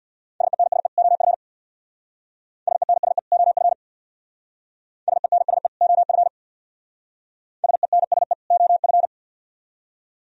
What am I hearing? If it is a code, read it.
Morse code, heard as a keyed tone, about 64 wpm, 700 Hz. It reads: SENSE OF SENSE OF SENSE OF SENSE OF